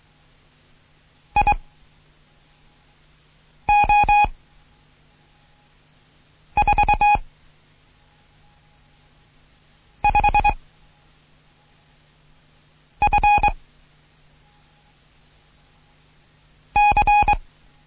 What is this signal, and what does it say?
Morse code, heard as a keyed tone, about 23 wpm, 830 Hz. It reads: I O 4 5 F C
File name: Morsen / Kurs A 16